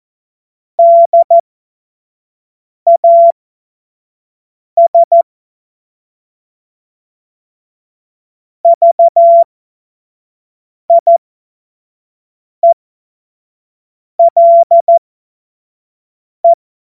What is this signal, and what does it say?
Morse code, heard as a keyed tone, about 14 wpm, spaced as for 5 wpm, 680 Hz. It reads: DAS VIELE
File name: Morsekurs / Witze / Witz 25. Januar 2022, 25 F70